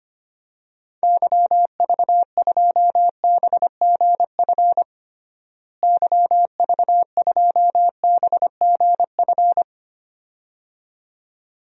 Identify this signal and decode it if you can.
Morse code, heard as a keyed tone, about 25 wpm, 700 Hz. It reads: YV2BGF YV2BGF
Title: DXCC-100 Call Signs - 6 Characters - Repeated 2x - 25wpm